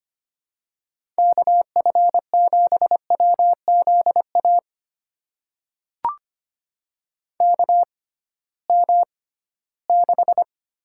Morse code, heard as a keyed tone, about 25 wpm, spaced as for 16 wpm, 700 Hz. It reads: KF7WZA KM6
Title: USA Call Signs - 6 Characters - 6x Character Spacing - 25wpm